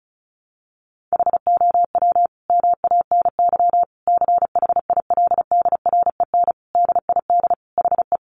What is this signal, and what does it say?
Morse code, heard as a keyed tone, about 34 wpm, 700 Hz. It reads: HOW MANY CHILDREN DID HE
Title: Sentences from Top 400 Words 35wpm